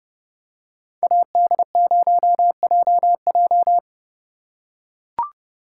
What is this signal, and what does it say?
Morse code, heard as a keyed tone, about 29 wpm, 700 Hz. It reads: AD0JJ